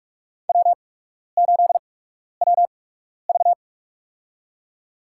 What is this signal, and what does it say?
Morse code, heard as a keyed tone, about 46 wpm, spaced as for 12 wpm, 700 Hz. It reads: W8WV
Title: USA Call Signs - 8x Character Spacing - 45wpm